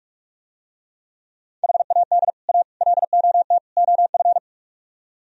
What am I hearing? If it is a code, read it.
Morse code, heard as a keyed tone, about 44 wpm, 700 Hz. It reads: HAD A LOT OF